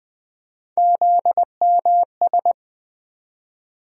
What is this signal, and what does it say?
Morse code, heard as a keyed tone, about 20 wpm, 700 Hz. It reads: ZMS
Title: USA Call Sign Postfixes - 3 Letters 20wpm